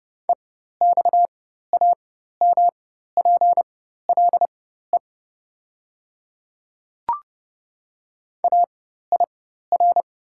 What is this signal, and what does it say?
Morse code, heard as a keyed tone, about 32 wpm, spaced as for 14 wpm, 700 Hz. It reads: EXAMPLE AIR
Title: 4x Character Spacing - Top 500 Words - 30wpm